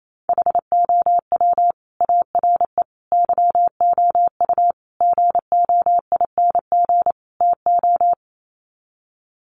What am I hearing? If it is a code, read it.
Morse code, heard as a keyed tone, about 28 wpm, 700 Hz. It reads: HOW ARE YOU GOING TO
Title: Sets of 5 Words 28wpm